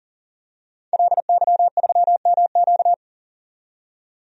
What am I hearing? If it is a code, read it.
Morse code, heard as a keyed tone, about 40 wpm, 700 Hz. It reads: LY3MQ
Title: DXCC-100 Call Signs - 5 Characters - 40wpm